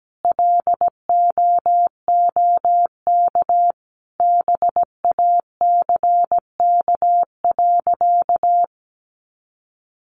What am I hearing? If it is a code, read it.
Morse code, heard as a keyed tone, about 17 wpm, 700 Hz. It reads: LOOK BACK.
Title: Sentences from Top 300 Words 17wpm